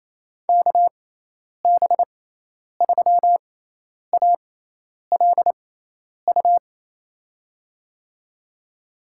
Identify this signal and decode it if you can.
Morse code, heard as a keyed tone, about 28 wpm, spaced as for 10 wpm, 700 Hz. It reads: KB3ALU
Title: USA Call Signs - 6 Characters - 6x Character Spacing - 28wpm